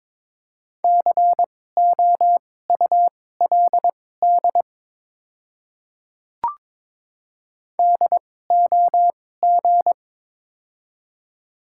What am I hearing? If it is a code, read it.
Morse code, heard as a keyed tone, about 22 wpm, spaced as for 16 wpm, 700 Hz. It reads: COULD DOG
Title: 2x Character Spacing - Top 500 Words - 22wpm